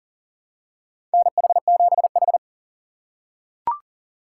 Morse code, heard as a keyed tone, about 40 wpm, 700 Hz. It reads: NH7H
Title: USA Call Signs - 4 Characters - 6x Character Spacing - 40wpm